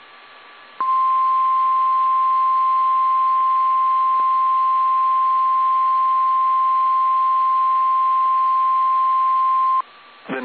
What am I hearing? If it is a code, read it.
Emergency Alert System (SAME) tones (the 1050 Hz attention tone). It is heard from 0.7 to 9.9 s.
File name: eas / KSDZKDJL / KSDZKDJL